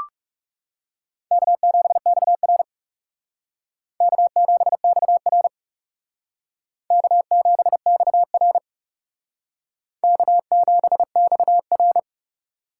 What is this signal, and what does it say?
Morse code, heard as a keyed tone, about 37 wpm, 700 Hz. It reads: K7XR K7XR K7XR K7XR
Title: Speed Racing - USA 1x2 Call Signs 30wpm